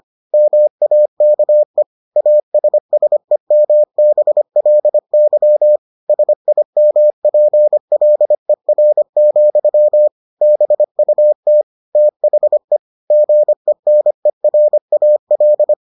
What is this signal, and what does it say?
Morse code, heard as a keyed tone, about 25 wpm, 600 Hz. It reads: MAKE ASSEMBLY SIMPLER, BUT THE GENERAL